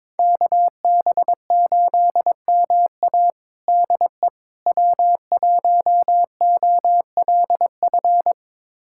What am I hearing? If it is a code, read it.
Morse code, heard as a keyed tone, about 22 wpm, 700 Hz. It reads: KB8MA DE W1OLF
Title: USA Callsign DE Callsign - 5 Characters - 22wpm